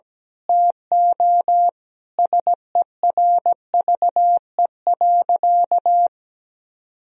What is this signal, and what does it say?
Morse code, heard as a keyed tone, about 17 wpm, 700 Hz. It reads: TO SERVE.
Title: Sentences from Top 1000 Words - Review Part 4 - 17wpm